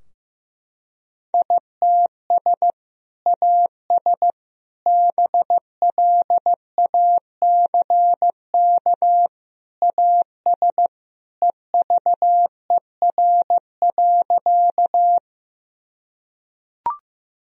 Morse code, heard as a keyed tone, about 15 wpm, 700 Hz. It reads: ITS AS BLACK AS EVER.
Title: Sentences from Top 1000 Words - Review Part 2 - 15wpm